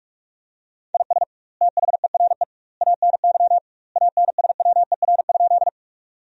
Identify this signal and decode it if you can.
Morse code, heard as a keyed tone, about 48 wpm, 700 Hz. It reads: IS THERE ANY ANSWER?
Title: 2x Word-Spacing - Sentences from Top 300 Words - 45wpm